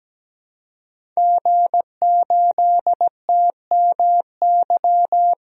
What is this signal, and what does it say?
Morse code, heard as a keyed tone, about 17 wpm, 700 Hz. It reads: G8TMY
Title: DXCC-100 Call Signs - 5 Characters - 17wpm